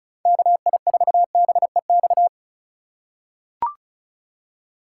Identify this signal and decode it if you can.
Morse code, heard as a keyed tone, about 34 wpm, 700 Hz. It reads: KI4BEX
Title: USA Call Signs - 2x Character Spacing - 35wpm